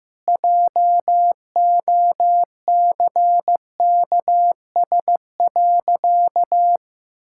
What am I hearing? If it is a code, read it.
Morse code, heard as a keyed tone, about 15 wpm, 700 Hz. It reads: JOCKS.
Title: Pangram Challenge v2 - 15wpm